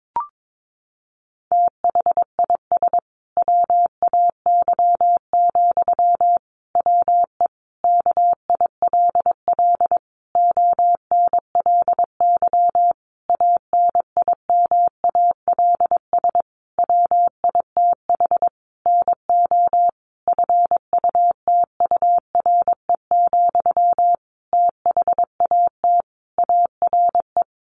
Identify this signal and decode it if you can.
Morse code, heard as a keyed tone, about 22 wpm, 700 Hz. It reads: THIS WAY, WE KILL ONLY ANIMALS WITH NO FUTURE, THAT ARE
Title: A Sound Of Thunder 22wpm